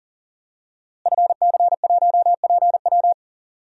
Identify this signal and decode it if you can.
Morse code, heard as a keyed tone, about 40 wpm, 700 Hz. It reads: FC1PW
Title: DXCC-100 Call Signs - 5 Characters - Repeated 3x - 40wpm